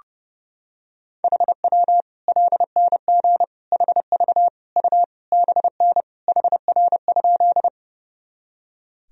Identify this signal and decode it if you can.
Morse code, heard as a keyed tone, about 30 wpm, 700 Hz. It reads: HW LNG HV U BN HR?
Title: Sentences from Top 200 Words and Abbreviations 30wpm